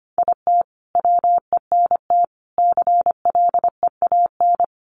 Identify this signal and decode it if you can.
Morse code, heard as a keyed tone, about 25 wpm, 700 Hz. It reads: IT WENT CLEAN
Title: Sentences from Top 1200 Words 25wpm